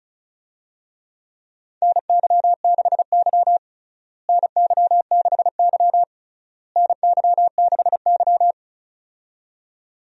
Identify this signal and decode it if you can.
Morse code, heard as a keyed tone, about 36 wpm, 700 Hz. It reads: NY6Y NY6Y NY6Y
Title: USA Call Signs - 4 Characters - Repeated 3x - 35wpm